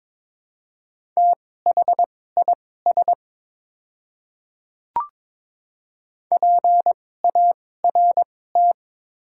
Morse code, heard as a keyed tone, about 22 wpm, spaced as for 16 wpm, 700 Hz. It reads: THIS PART
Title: 2x Character Spacing - Top 300 Words - 22wpm